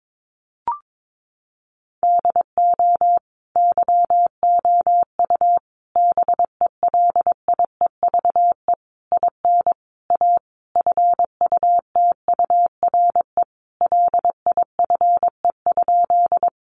Morse code, heard as a keyed tone, about 22 wpm, 700 Hz. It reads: DO YOU BELIEVE IN A FUTURE LIFE?
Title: Sentences from Top 1000 Words 22wpm